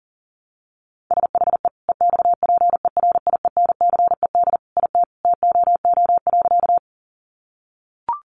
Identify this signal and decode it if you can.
Morse code, heard as a keyed tone, about 40 wpm, 700 Hz. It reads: SHE EXPERIENCED IT TOO.
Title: Sentences from Top 1100 Words 40wpm